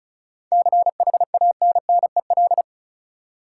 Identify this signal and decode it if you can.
Morse code, heard as a keyed tone, about 34 wpm, 700 Hz. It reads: CHANNEL
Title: Seven Letter Words - 35wpm